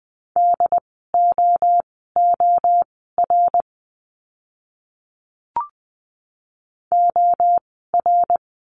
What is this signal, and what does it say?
Morse code, heard as a keyed tone, about 20 wpm, spaced as for 14 wpm, 700 Hz. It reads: DOOR OR
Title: 101 Lesson 17f - Letter u - Words 20wpm